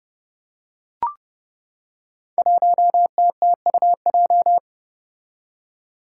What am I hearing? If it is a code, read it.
Morse code, heard as a keyed tone, about 30 wpm, 700 Hz. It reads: 1TTUJ
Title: Five Challenging Letter-Number Combinations 30wpm